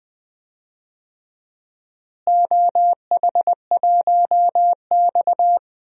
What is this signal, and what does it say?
Morse code, heard as a keyed tone, about 20 wpm, 700 Hz. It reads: OH1X
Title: DXCC-100 Call Signs - 4 Characters - Repeated 3x - 20wpm